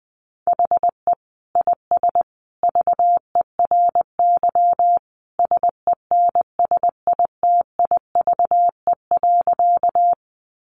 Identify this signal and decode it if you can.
Morse code, heard as a keyed tone, about 20 wpm, 700 Hz. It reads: HE IS VERY SENSITIVE.
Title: Sentences from Top 1400 Words 20wpm